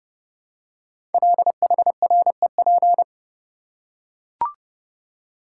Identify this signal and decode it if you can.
Morse code, heard as a keyed tone, about 30 wpm, 690 Hz. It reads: LHREP
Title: Code Groups - 5 Letters - 30wpm